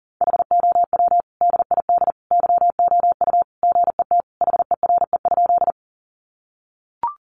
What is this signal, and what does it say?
Morse code, heard as a keyed tone, about 40 wpm, 700 Hz. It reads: HOW DID YOU GET HERE?
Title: Sentences from Top 200 Words 40wpm